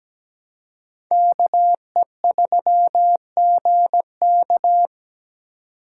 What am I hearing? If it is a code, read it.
Morse code, heard as a keyed tone, about 21 wpm, 700 Hz. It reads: KE3GK
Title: Speed Racing - USA Call Signs 17wpm